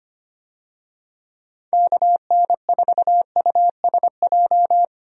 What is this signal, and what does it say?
Morse code, heard as a keyed tone, about 25 wpm, 700 Hz. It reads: KN4USJ
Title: USA Call Signs - 6 Characters - Repeated 3x - 25wpm